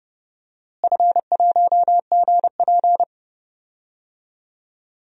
Morse code, heard as a keyed tone, about 31 wpm, 700 Hz. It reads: F1GP
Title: DXCC-100 Call Signs - 4 Characters - Repeated 3x - 30wpm